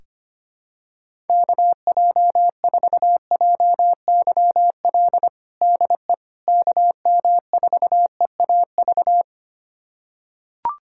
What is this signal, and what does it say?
Morse code, heard as a keyed tone, about 25 wpm, 700 Hz. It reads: KJ4JYL DE KM4EAV
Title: USA Callsign DE Callsign 25wpm